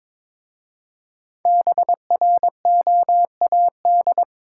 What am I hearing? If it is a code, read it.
Morse code, heard as a keyed tone, about 22 wpm, 690 Hz. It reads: BROAD
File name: 5 Letter Words - First 3 Letters Given - 22wpm